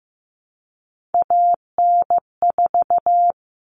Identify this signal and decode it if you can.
Morse code, heard as a keyed tone, about 15 wpm, 700 Hz. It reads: AN4
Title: All Cut Numbers - 3-Digit 15wpm